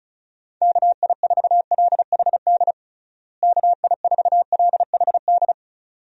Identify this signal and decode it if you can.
Morse code, heard as a keyed tone, about 36 wpm, 700 Hz. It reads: KI4LHD KI4LHD
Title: USA Call Signs - 6 Characters - Repeated 2x - 35wpm